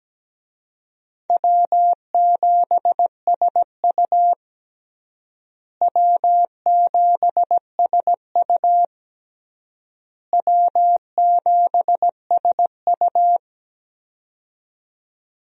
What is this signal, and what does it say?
Morse code, heard as a keyed tone, about 17 wpm, 700 Hz. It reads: W7SU W7SU W7SU
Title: USA Call Signs - 4 Characters - Repeated 3x - 17wpm